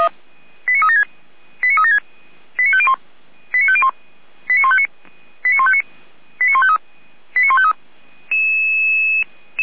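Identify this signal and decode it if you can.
Five-tone selective call (ZVEI), heard as quick runs of five tones, tones of about 70 ms. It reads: 89376 89376 89501 89501 89159 89159 89153 89153
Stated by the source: ZVEI five-tone selective call received by SDR.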